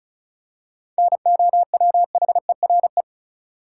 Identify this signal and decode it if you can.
Morse code, heard as a keyed tone, about 36 wpm, 700 Hz. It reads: NOWHERE
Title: Seven Letter Words - 35wpm